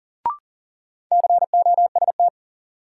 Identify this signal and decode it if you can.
Morse code, heard as a keyed tone, about 40 wpm, 700 Hz. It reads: COST
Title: Sets of 3 Words - New Word Drill - 40wpm